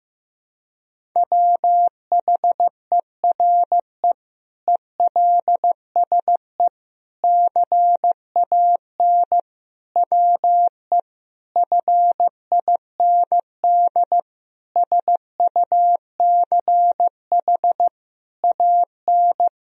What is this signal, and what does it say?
Morse code, heard as a keyed tone, about 15 wpm, 700 Hz. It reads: WHERE ELSE CAN WE FIND SUCH AN
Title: Sentences from Top 1000 Words - Review Part 1 - 15wpm